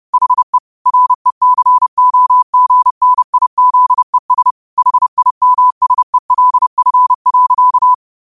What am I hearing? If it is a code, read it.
Morse code, heard as a keyed tone, about 30 wpm, 1000 Hz. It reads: HE RECOGNIZES HIMSELF.
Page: New quotes every day in morse code at 30 Words per minute.